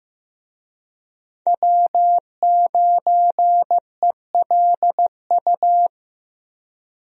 Morse code, heard as a keyed tone, about 15 wpm, 700 Hz. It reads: W9ELU
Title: USA Call Signs - 5 Characters - Repeated 2x - 15wpm